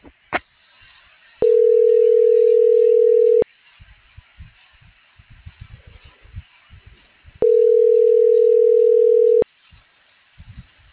Ringback tone (North America), 440 + 480 Hz.
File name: CENTRAL NAVIERAS / out alejandro.cruz